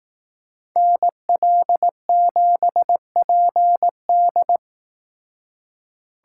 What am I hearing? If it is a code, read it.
Morse code, heard as a keyed tone, about 18 wpm, 700 Hz. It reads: NL7PD
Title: Speed Racing - USA Call Signs 15wpm